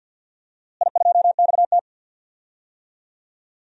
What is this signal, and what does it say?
Morse code, heard as a keyed tone, about 51 wpm, 700 Hz. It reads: I2XT
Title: DXCC-100 Call Signs - 4 Characters - Repeated 2x - 50wpm